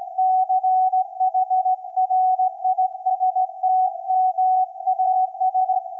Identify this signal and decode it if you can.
Morse code, heard as a keyed tone, about 16 wpm, 750 Hz. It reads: CHRISTMAS
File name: iz7slz / VLF / SAQ 24-12-2024